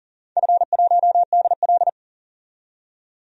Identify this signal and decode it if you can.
Morse code, heard as a keyed tone, about 40 wpm, 700 Hz. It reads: F1DL
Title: DXCC-100 Call Signs - 4 Characters - 40wpm